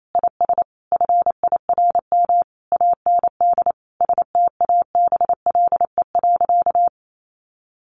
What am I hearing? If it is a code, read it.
Morse code, heard as a keyed tone, about 28 wpm, 700 Hz. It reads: IS FIRM AND STABLE.
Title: Sentences from Top 1500 Words 28wpm